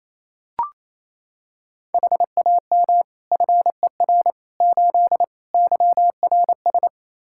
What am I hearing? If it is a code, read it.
Morse code, heard as a keyed tone, about 27 wpm, 700 Hz. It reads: HAM FER 8 YRS
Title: Practice QSO - Exchange 3 - More Info - 28wpm